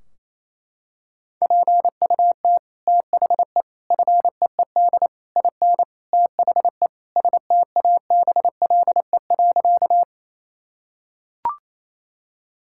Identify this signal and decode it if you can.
Morse code, heard as a keyed tone, about 28 wpm, 700 Hz. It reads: PUT THE FEED IN THE STABLE.
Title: Sentences from Top 1600 Words 28wpm